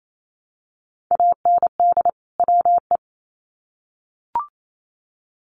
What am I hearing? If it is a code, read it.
Morse code, heard as a keyed tone, about 28 wpm, 700 Hz. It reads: AND WE
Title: Sets of 2 Words 28wpm